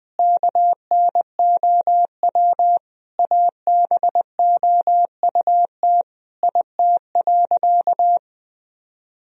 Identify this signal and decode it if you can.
Morse code, heard as a keyed tone, about 21 wpm, 700 Hz. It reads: KNOW ABOUT IT.
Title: Sentences from Top 1000 Words - Review Part 1 - 20wpm